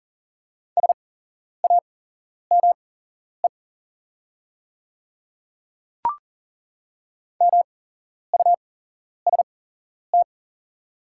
Morse code, heard as a keyed tone, about 40 wpm, spaced as for 11 wpm, 700 Hz. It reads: SAME MUST
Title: 8x Character Spacing - Top 200 Words - 40wpm